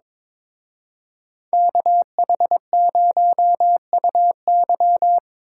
Morse code, heard as a keyed tone, about 26 wpm, 700 Hz. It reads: KH0UY
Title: Speed Racing - USA 2x2 Call Signs 22wpm